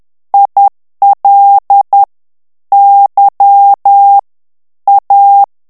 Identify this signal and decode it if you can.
Morse code, heard as a keyed tone, about 10 wpm, 800 Hz. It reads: IL Y A
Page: Français - texte en clair 1 - vitesse : 10 mots minute